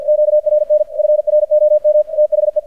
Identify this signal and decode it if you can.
Morse code, heard as a keyed tone, about 50 wpm, 600 Hz. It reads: QRN FROM THE
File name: archived / CRSnet